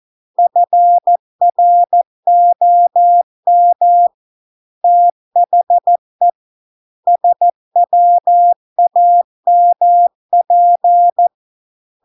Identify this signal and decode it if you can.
Morse code, heard as a keyed tone, about 14 wpm, 700 Hz. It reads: FROM THE SWAMP